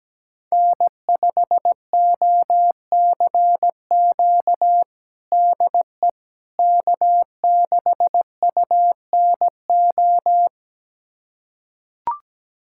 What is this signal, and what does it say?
Morse code, heard as a keyed tone, about 17 wpm, 700 Hz. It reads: N5OCQ DE K6UNO
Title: USA Callsign DE Callsign - 5 Characters - 17wpm